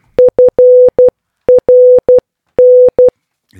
Morse code, heard as a keyed tone, about 12 wpm, 500 Hz. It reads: FRN